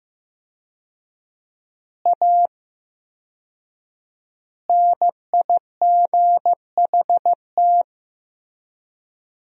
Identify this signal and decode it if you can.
Morse code, heard as a keyed tone, about 14 wpm, 700 Hz. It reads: A NIGHT
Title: 4x Word-Spacing - Sentences from Top 300 Words - 15wpm